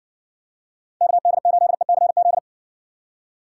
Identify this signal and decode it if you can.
Morse code, heard as a keyed tone, about 53 wpm, 700 Hz. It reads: KD8ECZ